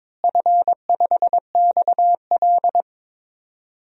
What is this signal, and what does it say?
Morse code, heard as a keyed tone, about 22 wpm, 700 Hz. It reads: F5XL
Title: DXCC-100 Call Signs - 4 Characters - Repeated 3x - 22wpm